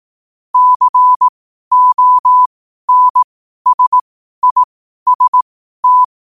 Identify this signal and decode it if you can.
Morse code, heard as a keyed tone, about 18 wpm, spaced as for 12 wpm, 1000 Hz. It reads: CONSIST
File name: qotdfiles / 12wpm